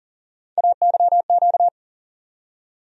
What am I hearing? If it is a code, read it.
Morse code, heard as a keyed tone, about 40 wpm, 700 Hz. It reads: AYQ
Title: USA Call Sign Postfixes - 3 Letters 40wpm